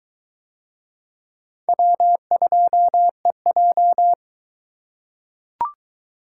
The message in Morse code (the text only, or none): W2EJ